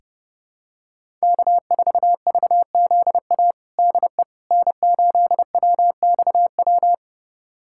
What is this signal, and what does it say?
Morse code, heard as a keyed tone, about 30 wpm, 700 Hz. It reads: K4VZA DE N8WXW
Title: USA Callsign DE Callsign - 5 Characters - 30wpm